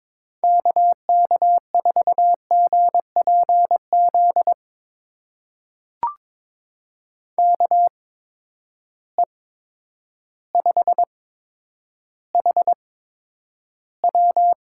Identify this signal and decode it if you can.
Morse code, heard as a keyed tone, about 22 wpm, spaced as for 10 wpm, 700 Hz. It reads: KK4GPZ KE5HW
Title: USA Call Signs - 6 Characters - 8x Character Spacing - 22wpm